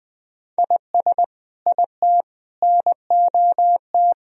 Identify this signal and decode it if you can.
Morse code, heard as a keyed tone, about 21 wpm, 700 Hz. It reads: IS IT NOT
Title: Sentences from Top 1000 Words - Review Part 2 - 20wpm